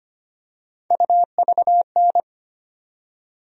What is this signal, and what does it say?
Morse code, heard as a keyed tone, about 25 wpm, 700 Hz. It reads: UVN
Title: USA Call Sign Postfixes - 3 Letters 25wpm